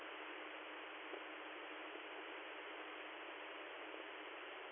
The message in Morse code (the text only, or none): none